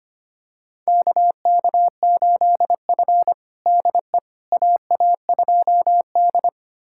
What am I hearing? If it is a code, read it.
Morse code, heard as a keyed tone, about 25 wpm, 700 Hz. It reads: KK8F DE AA2D
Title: USA Callsign DE Callsign - 4 Characters - 25wpm